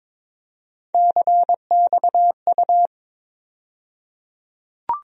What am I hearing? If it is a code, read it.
Morse code, heard as a keyed tone, about 22 wpm, 700 Hz. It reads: CXU
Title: USA Call Sign Postfixes - 3 Letters 22wpm